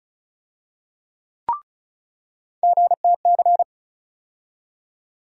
Morse code, heard as a keyed tone, about 36 wpm, 700 Hz. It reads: GTC
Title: USA Call Sign Postfixes - 3 Letters 35wpm